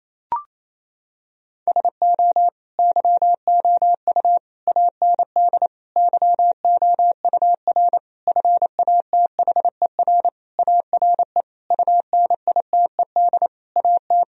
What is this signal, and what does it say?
Morse code, heard as a keyed tone, about 28 wpm, 700 Hz. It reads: SO YOU AND YOUR FATHER ARE UNITED AT
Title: Sentences from Top 1000 Words - Review Part 4 - 28wpm